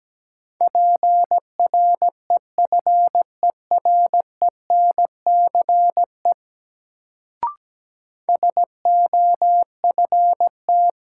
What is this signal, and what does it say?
Morse code, heard as a keyed tone, about 17 wpm, 700 Hz. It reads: PREFERENCE SOFT
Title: Sentences from Top 1500 Words - New Word Drill Alt - 17wpm